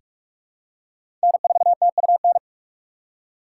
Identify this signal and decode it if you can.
Morse code, heard as a keyed tone, about 46 wpm, 700 Hz. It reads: N4TUN
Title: USA Call Signs - 5 Characters - 4x Character Spacing - 45wpm